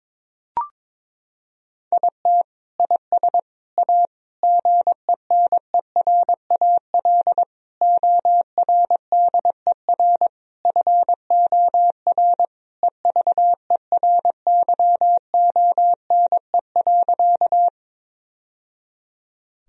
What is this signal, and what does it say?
Morse code, heard as a keyed tone, about 22 wpm, 700 Hz. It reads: IT IS A GENERAL ORDER FOR EVERYONE.
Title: Sentences from Top 1000 Words - Review Part 3 - 22wpm